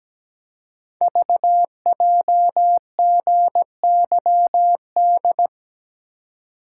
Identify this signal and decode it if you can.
Morse code, heard as a keyed tone, about 17 wpm, 700 Hz. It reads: VJGYD
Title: Code Groups - 5 Letters - 17wpm